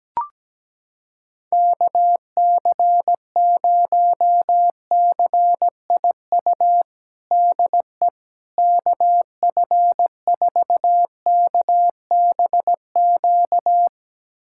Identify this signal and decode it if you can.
Morse code, heard as a keyed tone, about 17 wpm, 700 Hz. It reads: KC0CIU DE KF4KBQ
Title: USA Callsign DE Callsign - 6 Characters - 17wpm